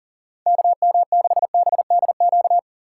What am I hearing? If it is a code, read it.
Morse code, heard as a keyed tone, about 40 wpm, 700 Hz. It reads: KM6BDQ
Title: USA Call Signs - 4x Character Spacing - 40wpm